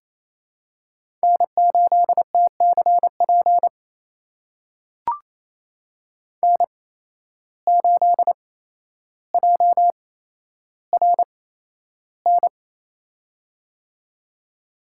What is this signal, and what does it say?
Morse code, heard as a keyed tone, about 28 wpm, spaced as for 12 wpm, 700 Hz. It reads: N8TCP N8JRN
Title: USA Call Signs - 5 Characters - 8x Character Spacing - 28wpm